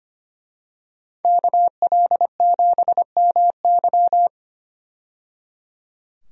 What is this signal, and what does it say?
Morse code, heard as a keyed tone, about 25 wpm, 700 Hz. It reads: KL7MY
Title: USA Call Signs - 5 Characters - 25wpm